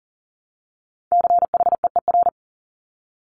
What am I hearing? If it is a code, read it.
Morse code, heard as a keyed tone, about 39 wpm, 700 Hz. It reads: CHEER
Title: Multi-letter Phonemes Repeated 3x 40wpm